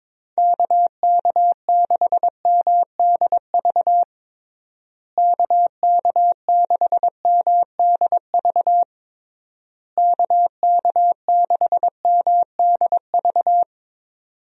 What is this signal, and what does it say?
Morse code, heard as a keyed tone, about 21 wpm, 700 Hz. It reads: KK6MDV KK6MDV KK6MDV
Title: USA Call Signs - 6 Characters - Repeated 3x - 22wpm